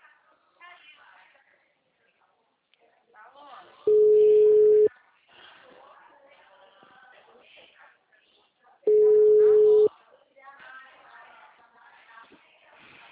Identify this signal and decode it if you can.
Ringback tone (Europe, 425 Hz), with speech.